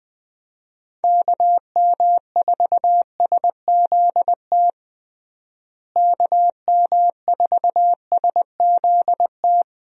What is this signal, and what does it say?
Morse code, heard as a keyed tone, about 20 wpm, 700 Hz. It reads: KM4SZT KM4SZT
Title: USA Call Signs - 6 Characters - Repeated 3x - 20wpm